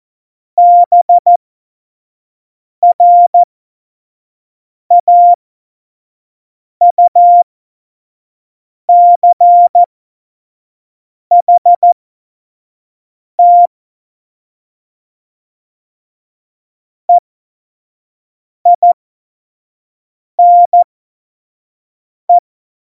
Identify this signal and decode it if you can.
Morse code, heard as a keyed tone, about 14 wpm, spaced as for 5 wpm, 700 Hz. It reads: BRAUCHT EINE
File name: Morsekurs / Witze / Witz 13. Oktober 2020, 25 F70